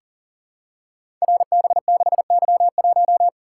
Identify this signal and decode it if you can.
Morse code, heard as a keyed tone, about 41 wpm, 700 Hz. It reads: RB6Y1
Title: Code Groups - 5 Letter:Numbers - 40wpm